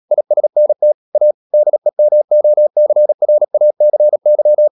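Morse code, heard as a keyed tone, about 37 wpm, 600 Hz. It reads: ISNT A DEMOCRACY
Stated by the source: Morse code practice at 35–39 wpm sending English text of Ken Follett's novel Never, from CW med Gnister.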